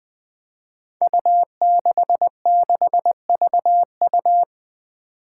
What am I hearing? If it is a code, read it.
Morse code, heard as a keyed tone, about 21 wpm, 700 Hz. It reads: U66VU
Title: Five Challenging Letter-Number Combinations 20wpm